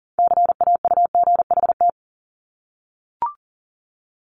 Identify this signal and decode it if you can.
Morse code, heard as a keyed tone, about 40 wpm, 700 Hz. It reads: CAUGHT
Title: Multi-letter Phonemes 40wpm